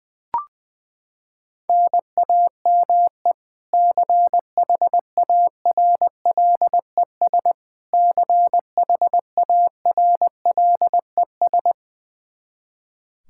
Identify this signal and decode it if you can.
Morse code, heard as a keyed tone, about 20 wpm, 700 Hz. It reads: NAME CHARLES CHARLES
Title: Practice QSO - Exchange 1 - Beginning - 20wpm